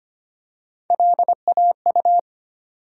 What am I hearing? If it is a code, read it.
Morse code, heard as a keyed tone, about 25 wpm, 700 Hz. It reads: LAU
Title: USA Call Sign Postfixes - 3 Letters 25wpm